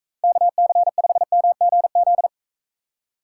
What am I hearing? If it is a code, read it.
Morse code, heard as a keyed tone, about 42 wpm, 700 Hz. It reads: KC5MGZ